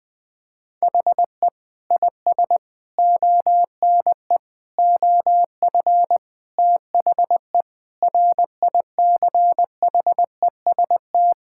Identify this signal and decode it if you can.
Morse code, heard as a keyed tone, about 20 wpm, 700 Hz. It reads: HE IS ONE OF THE RICHEST
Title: Sentences with 8-Word Phrases - Spoken Beginning - 20wpm